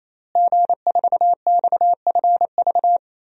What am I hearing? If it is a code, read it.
Morse code, heard as a keyed tone, about 28 wpm, 700 Hz. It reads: G4XFV
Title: DXCC-100 Call Signs - 5 Characters - Repeated 3x - 28wpm